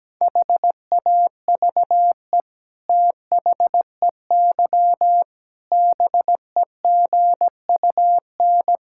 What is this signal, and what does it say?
Morse code, heard as a keyed tone, about 17 wpm, 700 Hz. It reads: HAVE THEY BEGUN
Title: Sentences from Top 1000 Words - Review Part 6 - 17wpm